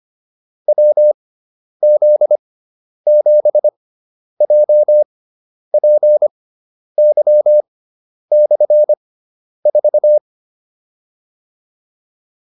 Morse code, heard as a keyed tone, about 25 wpm, spaced as for 10 wpm, 600 Hz. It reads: WZ7JPY/4
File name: QSO309 10